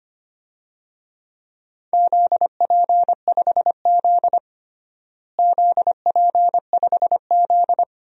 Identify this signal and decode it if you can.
Morse code, heard as a keyed tone, about 25 wpm, 700 Hz. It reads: ZP5Z ZP5Z
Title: DXCC-100 Call Signs - 4 Characters - Repeated 2x - 25wpm